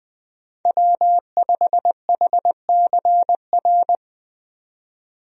Morse code, heard as a keyed tone, about 20 wpm, 700 Hz. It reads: W5HCR
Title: USA Call Signs - 5 Characters - 20wpm